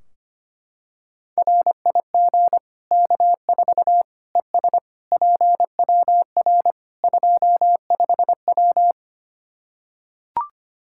Morse code, heard as a keyed tone, about 25 wpm, 700 Hz. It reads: RIG K4 ES PWR 25W
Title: Practice QSO - Exchange 2 - Info - 25wpm